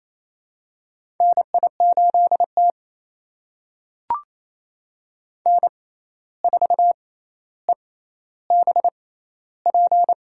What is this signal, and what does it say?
Morse code, heard as a keyed tone, about 28 wpm, spaced as for 13 wpm, 700 Hz. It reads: NI8T N4EBP
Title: USA Call Signs - 6x Character Spacing - 28wpm